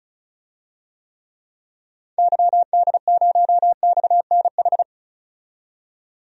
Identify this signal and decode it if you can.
Morse code, heard as a keyed tone, about 35 wpm, 700 Hz. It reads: YD0XNH
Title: DXCC-100 Call Signs - 6 Characters - 35wpm